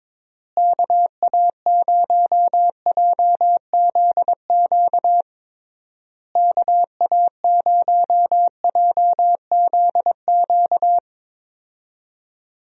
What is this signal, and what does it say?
Morse code, heard as a keyed tone, about 22 wpm, 700 Hz. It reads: KA0JZQ KA0JZQ
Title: USA Call Signs - 6 Characters - Repeated 3x - 22wpm